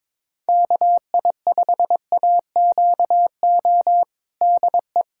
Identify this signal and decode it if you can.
Morse code, heard as a keyed tone, about 22 wpm, 700 Hz. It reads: KI5AQO DE
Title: USA Callsign DE Callsign 22wpm